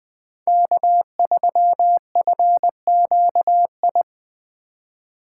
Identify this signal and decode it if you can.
Morse code, heard as a keyed tone, about 19 wpm, 700 Hz. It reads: K3FQI